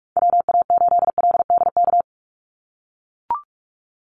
Morse code, heard as a keyed tone, about 44 wpm, 700 Hz. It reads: WA8LDK
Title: USA Call Signs 45wpm